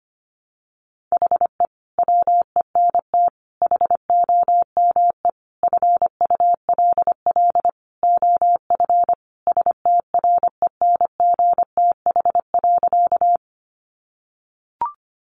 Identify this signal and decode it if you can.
Morse code, heard as a keyed tone, about 25 wpm, 700 Hz. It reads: HE WENT HOME FULL OF STRENGTH.
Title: Sentences from Top 1300 Words 25wpm